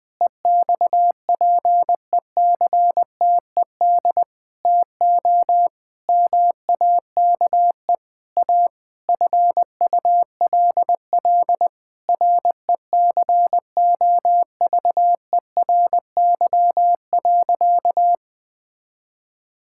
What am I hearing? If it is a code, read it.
Morse code, heard as a keyed tone, about 20 wpm, 700 Hz. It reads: EXPECTED TO MAKE A FULL RECOVERY.
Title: Sentences with 7-Word Phrases - 20wpm